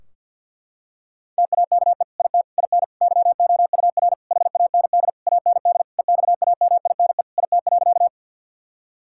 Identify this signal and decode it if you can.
Morse code, heard as a keyed tone, about 50 wpm, 700 Hz. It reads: TAKE IT IN YOUR HAND AND EXAMINE IT.
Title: Sentences from Top 1600 Words 50wpm